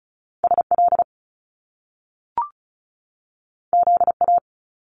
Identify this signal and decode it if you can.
Morse code, heard as a keyed tone, about 36 wpm, 700 Hz. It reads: SL ZA
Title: Top Two Letter Combinations in Words 35wpm